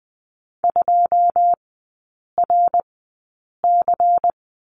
Morse code, heard as a keyed tone, about 20 wpm, spaced as for 8 wpm, 700 Hz. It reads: 2RC
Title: 100 Lesson 17e - Letter u - Three Characters 20wpm